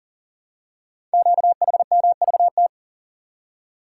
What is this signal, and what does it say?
Morse code, heard as a keyed tone, about 42 wpm, 690 Hz. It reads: QHMVT
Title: Code Groups - 5 Letters - 40wpm